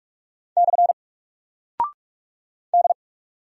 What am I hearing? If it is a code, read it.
Morse code, heard as a keyed tone, about 46 wpm, 700 Hz. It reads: / D
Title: Single Character (Letter:Number:Punctuation:Prosign) Alt 45wpm